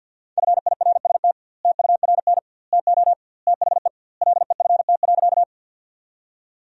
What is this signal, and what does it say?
Morse code, heard as a keyed tone, about 52 wpm, 700 Hz. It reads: FIRST TURN TO THE LEFT.
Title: 2x Word-Spacing - Sentences from Top 300 Words - 50wpm